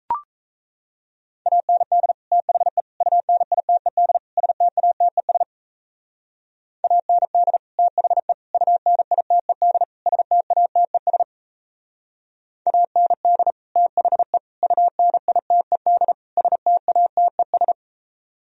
Morse code, heard as a keyed tone, about 34 wpm, 700 Hz. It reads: AND THE UNITED STATES AND THE UNITED STATES AND THE UNITED STATES
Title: Speed-Racing - Sets of 4 Words 28wpm